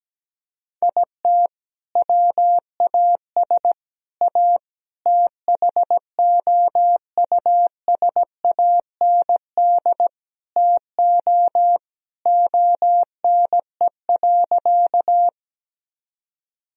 Morse code, heard as a keyed tone, about 17 wpm, 700 Hz. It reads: IT WAS A THOUSAND TO ONE.
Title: Sentences from Top 1000 Words - Review Part 3 - 17wpm